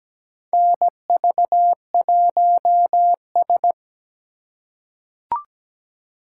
Morse code, heard as a keyed tone, about 17 wpm, 700 Hz. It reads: NV1S
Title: USA Call Signs - 4 Characters - 2x Character Spacing - 17wpm